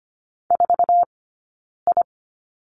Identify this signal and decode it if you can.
Morse code, heard as a keyed tone, about 25 wpm, spaced as for 9 wpm, 690 Hz. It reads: 4I
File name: course / 25:10wpm / 092 Lesson 16d - Letter c - Two Characters 25wpm